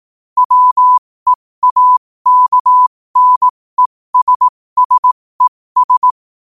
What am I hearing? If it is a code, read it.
Morse code, heard as a keyed tone, about 18 wpm, spaced as for 15 wpm, 1000 Hz. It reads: WEAKNESSES
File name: qotdfiles / 15wpm